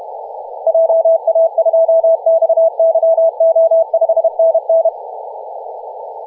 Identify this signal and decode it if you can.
Morse code, heard as a keyed tone, about 31 wpm, 660 Hz. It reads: JA2XYO 5NN
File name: posts / 2016-nov-23-0544z-xx9tkb-24-895mhz-cw